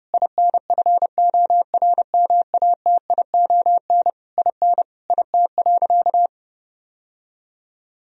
Morse code, heard as a keyed tone, about 30 wpm, 700 Hz. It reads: INFORMATION IN IT.
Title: Sentences from Top 1000 Words - Review Part 4 - 30wpm